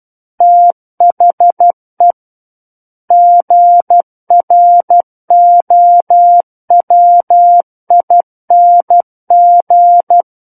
Morse code, heard as a keyed tone, about 12 wpm, 700 Hz. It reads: THE GROWING
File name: Book One / 03-Chapter 3 - 12 WPM